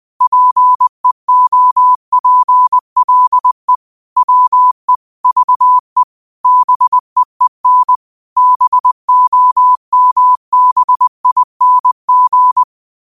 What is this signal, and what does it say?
Morse code, heard as a keyed tone, about 20 wpm, 1000 Hz. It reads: PEOPLE WE VE BEEN BOMBING
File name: 20wpm